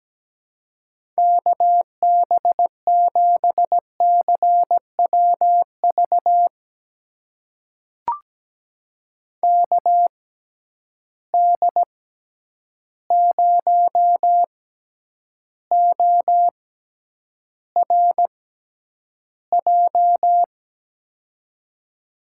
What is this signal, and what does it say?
Morse code, heard as a keyed tone, about 17 wpm, spaced as for 9 wpm, 700 Hz. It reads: KB7CWV KD0ORJ
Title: USA Call Signs - 6 Characters - 6x Character Spacing - 17wpm